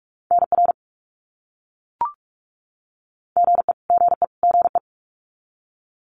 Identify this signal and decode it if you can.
Morse code, heard as a keyed tone, about 44 wpm, 700 Hz. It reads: NR GE GE GE
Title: Abbreviations Repeated 3x - 45wpm